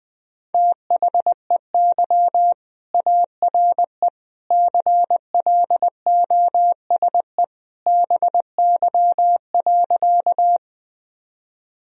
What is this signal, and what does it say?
Morse code, heard as a keyed tone, about 20 wpm, 700 Hz. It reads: THEY ARE CLOSE BY.
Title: Sentences from Top 1000 Words - Review Part 1 - 20wpm